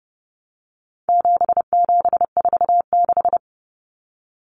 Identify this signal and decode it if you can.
Morse code, heard as a keyed tone, about 30 wpm, 700 Hz. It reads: 7746
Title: Numbers - 4-Digit 30wpm